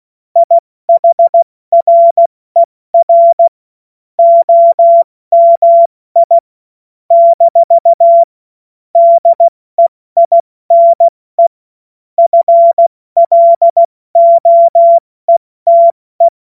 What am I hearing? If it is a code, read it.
Morse code, heard as a keyed tone, about 16 wpm, 680 Hz. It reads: IHRER OMI - DEINE FLOETE
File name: Morsekurs / Witze / Witz 16. Mai 2023 70 F80